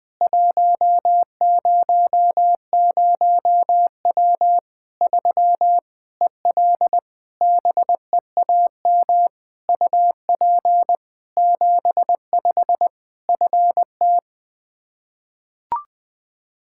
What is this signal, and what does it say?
Morse code, heard as a keyed tone, about 20 wpm, 700 Hz. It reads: 100W 3 EL BEAM UP 75 FT
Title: K1USN SST QSO Practice 20wpm